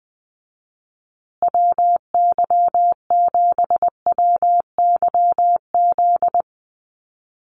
Morse code, heard as a keyed tone, about 20 wpm, 700 Hz. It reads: WY7WYZ
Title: USA Call Signs Repeated 2x 20wpm